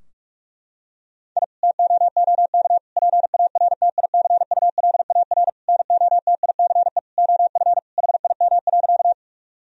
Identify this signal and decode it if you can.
Morse code, heard as a keyed tone, about 46 wpm, 700 Hz. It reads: I TOOK PARTICULAR NOTICE OF HIM.
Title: Sentences from Top 1000 Words - Review Part 4 - 45wpm